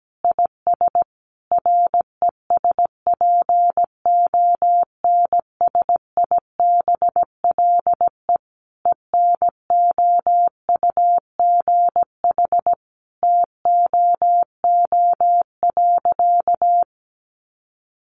Morse code, heard as a keyed tone, about 17 wpm, 700 Hz. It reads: IS RESPONSIBLE ENOUGH TOO.
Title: Additional Easy Phrases 17wpm